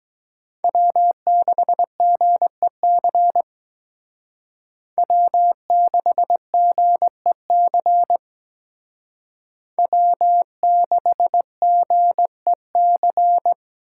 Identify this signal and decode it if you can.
Morse code, heard as a keyed tone, about 21 wpm, 700 Hz. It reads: W6GEC W6GEC W6GEC
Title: Speed Racing - USA 1x3 Call Signs 17wpm